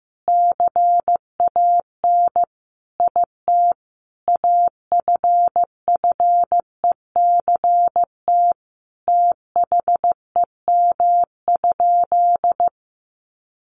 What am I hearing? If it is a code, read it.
Morse code, heard as a keyed tone, about 14 wpm, 700 Hz. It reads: CAN IT AFFECT THEM?
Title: Sentences from Top 900 Words 15wpm